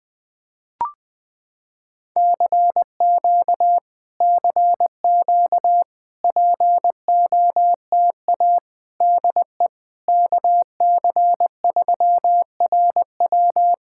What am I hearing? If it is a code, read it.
Morse code, heard as a keyed tone, about 21 wpm, 700 Hz. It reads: CQ CQ POTA DE KC3RW
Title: POTA QSO Practice - 20wpm